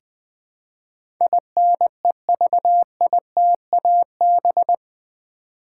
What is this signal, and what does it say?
Morse code, heard as a keyed tone, about 20 wpm, 700 Hz. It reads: INEVITAB
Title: Build A Word - 4 Letters At A Time - 10-Letter Words - 20wpm